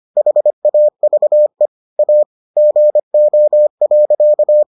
Morse code, heard as a keyed tone, about 25 wpm, 600 Hz. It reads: HAVE A GO.